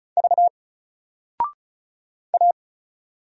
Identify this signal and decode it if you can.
Morse code, heard as a keyed tone, about 35 wpm, 700 Hz. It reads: V A
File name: Single-Letter-Number-35wpm